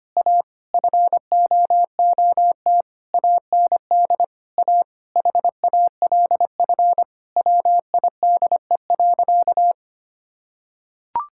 Morse code, heard as a keyed tone, about 25 wpm, 700 Hz. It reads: A FOOT AND A HALF WIDE.
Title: Sentences from Top 1000 Words - Review Part 6 - 25wpm